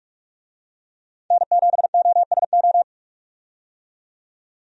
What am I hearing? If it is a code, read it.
Morse code, heard as a keyed tone, about 42 wpm, 700 Hz. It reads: N7OSO
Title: USA Call Signs - 5 Characters - 45wpm